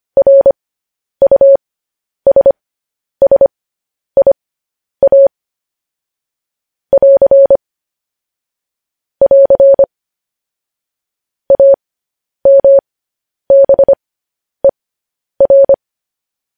Morse code, heard as a keyed tone, about 25 wpm, spaced as for 10 wpm, 550 Hz. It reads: RUSSIA <AR> <AR> AMBER
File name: podcast / News headlines-1x-25x10